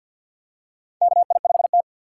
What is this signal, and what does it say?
Morse code, heard as a keyed tone, about 47 wpm, 700 Hz. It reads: KI5T
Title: USA Call Signs - 4 Characters - 6x Character Spacing - 50wpm